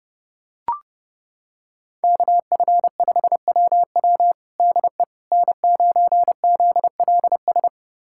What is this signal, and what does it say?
Morse code, heard as a keyed tone, about 30 wpm, 700 Hz. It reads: KF5WW DE N9ZLS
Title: USA Callsign DE Callsign - 5 Characters - 30wpm